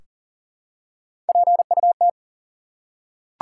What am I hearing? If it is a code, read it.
Morse code, heard as a keyed tone, about 42 wpm, 700 Hz. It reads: PUT
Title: Three Letter Words - 40wpm